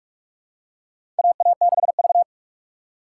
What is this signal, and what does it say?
Morse code, heard as a keyed tone, about 46 wpm, 700 Hz. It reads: AA6V
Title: USA Call Signs - 4 Characters - 4x Character Spacing - 45wpm